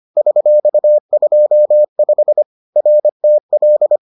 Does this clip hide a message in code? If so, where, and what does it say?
Yes: Morse code from 0.2 to 4.0 s, heard as a keyed tone, about 25 wpm, 600 Hz. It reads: $25 RTL